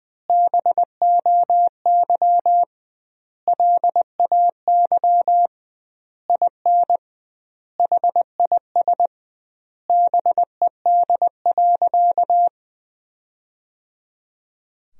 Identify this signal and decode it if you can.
Morse code, heard as a keyed tone, about 21 wpm, 700 Hz. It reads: BOY LAY IN HIS BED.
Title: 2x Word-Spacing - Sentences from Top 500 Words - 20wpm